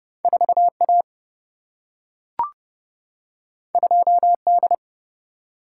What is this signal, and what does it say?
Morse code, heard as a keyed tone, about 30 wpm, 700 Hz. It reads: 4 A 2 D
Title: Single-Letter Single-Number 30wpm